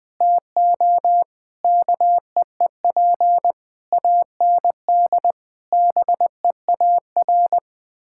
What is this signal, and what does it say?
Morse code, heard as a keyed tone, about 21 wpm, 700 Hz. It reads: TO KEEP AND BEAR
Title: Sets of 8 Words - 20wpm